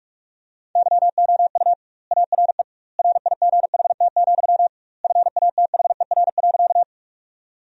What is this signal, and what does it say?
Morse code, heard as a keyed tone, about 43 wpm, 700 Hz. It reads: YOU ARE RIGHT, FATHER.
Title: 2x Word-Spacing - Sentences from Top 300 Words - 45wpm